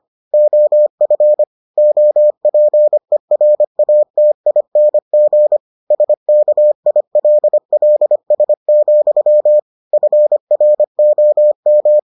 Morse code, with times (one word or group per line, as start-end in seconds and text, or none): 0.3-1.4 OF
1.8-5.6 OPERATING
5.9-9.6 SKILLS,
9.9-12.0 FROM